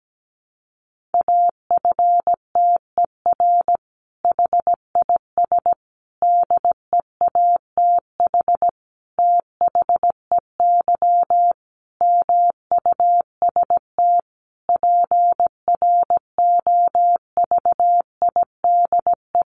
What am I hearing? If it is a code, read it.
Morse code, heard as a keyed tone, about 17 wpm, 700 Hz. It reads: AFTER HIS DEATH THEY MUST PROVIDE
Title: Sentences from Top 1100 Words 17wpm